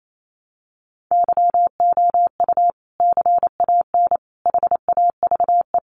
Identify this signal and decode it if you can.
Morse code, heard as a keyed tone, about 28 wpm, 700 Hz. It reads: YOU CAN HAVE
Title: Sentences from Top 100 Words 28wpm